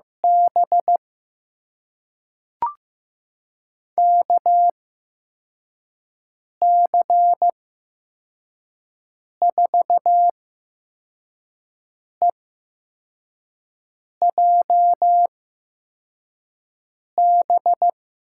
Morse code, heard as a keyed tone, about 15 wpm, spaced as for 4 wpm, 700 Hz. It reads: B KC4EJB
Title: USA Call Signs - 8x Character Spacing - 15wpm